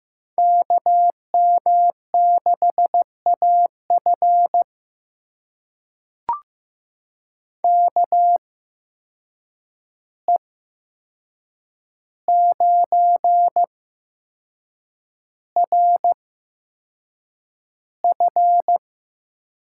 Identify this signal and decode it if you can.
Morse code, heard as a keyed tone, about 15 wpm, spaced as for 6 wpm, 700 Hz. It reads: KM6AF KE9RF
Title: USA Call Signs - 5 Characters - 8x Character Spacing - 15wpm